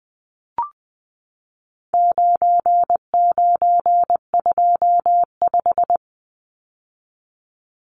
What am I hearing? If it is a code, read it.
Morse code, heard as a keyed tone, about 20 wpm, 700 Hz. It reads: 9925
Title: Numbers - 4-Digit 20wpm